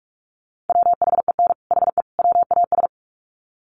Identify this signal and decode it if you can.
Morse code, heard as a keyed tone, about 44 wpm, 700 Hz. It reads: WHEN HE WAS
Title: Sets of 3 Words 45wpm